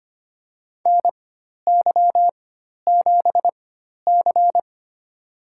Morse code, heard as a keyed tone, about 25 wpm, spaced as for 12 wpm, 700 Hz. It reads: NY7C
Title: USA Call Signs - 4 Characters - 4x Character Spacing - 25wpm